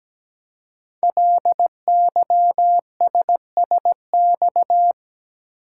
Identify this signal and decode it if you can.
Morse code, heard as a keyed tone, about 17 wpm, 700 Hz. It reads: LYSSX
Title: Code Groups - 5 Letters - 17wpm